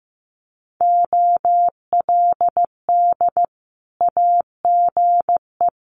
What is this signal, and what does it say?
Morse code, heard as a keyed tone, about 15 wpm, 700 Hz. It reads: OLD AGE
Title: Sentences from Top 1100 Words 15wpm